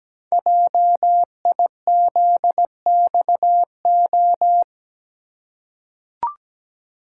Morse code, heard as a keyed tone, about 17 wpm, 700 Hz. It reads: JIZXO
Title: Code Groups - 5 Letters - 17wpm